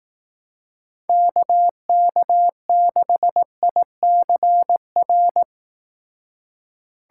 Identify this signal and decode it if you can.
Morse code, heard as a keyed tone, about 18 wpm, 700 Hz. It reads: KK6ICR
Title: Speed Racing - USA 2x3 Call Signs 15wpm